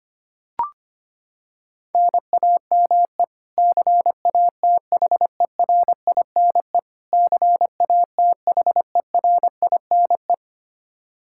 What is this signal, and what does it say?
Morse code, heard as a keyed tone, about 25 wpm, 700 Hz. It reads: NAME CATHERINE CATHERINE
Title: Practice QSO - Exchange 1 - Beginning - 25wpm